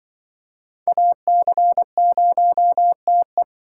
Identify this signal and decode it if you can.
Morse code, heard as a keyed tone, about 24 wpm, 700 Hz. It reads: AC0TE
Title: Speed Racing - USA Call Signs 20wpm